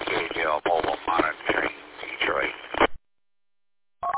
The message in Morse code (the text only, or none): none